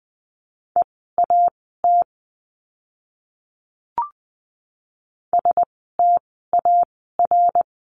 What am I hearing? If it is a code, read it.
Morse code, heard as a keyed tone, about 20 wpm, spaced as for 14 wpm, 700 Hz. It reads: EAT STAR
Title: 075 Lesson 13f - Letter L - Words 20wpm